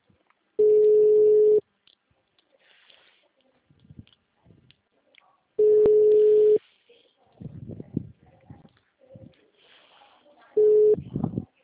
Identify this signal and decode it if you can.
Ringback tone (Europe, 425 Hz), with speech.
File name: out- -2006- - - .257